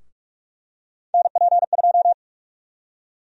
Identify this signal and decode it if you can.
Morse code, heard as a keyed tone, about 43 wpm, 700 Hz. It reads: NP2
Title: USA Call Sign Prefixes - Repeated 2x - 45wpm